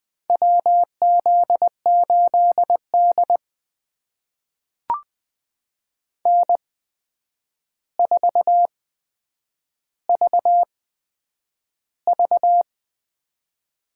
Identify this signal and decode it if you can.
Morse code, heard as a keyed tone, about 20 wpm, spaced as for 9 wpm, 700 Hz. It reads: WZ8D N4VV
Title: USA Call Signs - 4 Characters - 8x Character Spacing - 20wpm